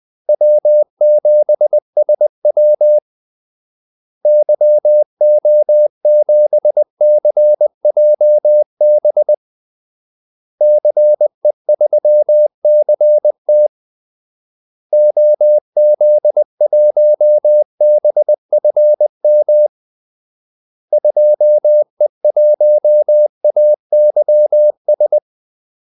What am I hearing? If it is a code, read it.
Morse code, heard as a keyed tone, about 20 wpm, 600 Hz. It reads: W7SW YO7CJB CE3CT OZ1BFM 2E1AYS
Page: Kaldesignaler 20 wpm